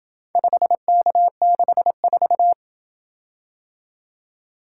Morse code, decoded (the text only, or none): 5K64